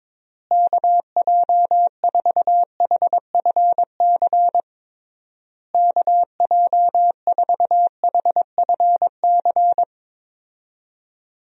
Morse code, heard as a keyed tone, about 22 wpm, 700 Hz. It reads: KJ4HFC KJ4HFC
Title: USA Call Signs - 6 Characters - Repeated 3x - 22wpm